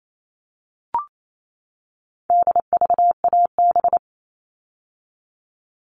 Morse code, heard as a keyed tone, about 28 wpm, 700 Hz. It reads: DVAB
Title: All Cut Numbers - 4-Digit 28wpm